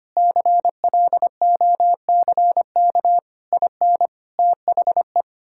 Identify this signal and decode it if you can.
Morse code, heard as a keyed tone, about 25 wpm, 700 Hz. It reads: CLOCK IN THE
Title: Pangram Challenge v2 - 25wpm